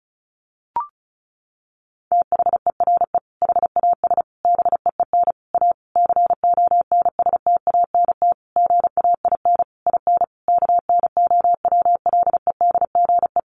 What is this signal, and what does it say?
Morse code, heard as a keyed tone, about 34 wpm, 700 Hz. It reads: THERE HAS BEEN A CONSTANT GAIN IN KNOWLEDGE
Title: Sentences from Top 1500 Words 35wpm